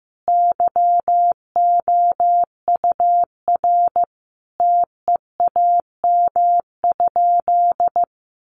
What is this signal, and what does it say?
Morse code, heard as a keyed tone, about 15 wpm, 700 Hz. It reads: YOUR TEAM?
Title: Sentences from Top 1500 Words 15wpm